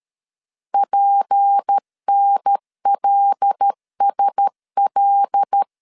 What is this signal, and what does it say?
Morse code, heard as a keyed tone, about 13 wpm, 800 Hz. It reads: PNLSL
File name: CW / DET / 1 6 1500C S A 12wpm 3440